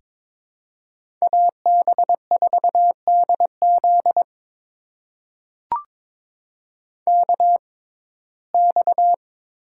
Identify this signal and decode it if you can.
Morse code, heard as a keyed tone, about 22 wpm, spaced as for 16 wpm, 700 Hz. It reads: AB4DZ KX
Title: USA Call Signs - 6x Character Spacing - 22wpm